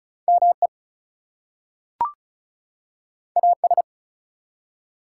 Morse code, heard as a keyed tone, about 36 wpm, 700 Hz. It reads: ME AS
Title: Two Letter Words - 35wpm